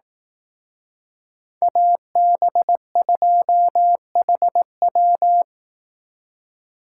Morse code, heard as a keyed tone, about 18 wpm, 700 Hz. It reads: AB2HW
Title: Speed Racing - USA 2x2 Call Signs 15wpm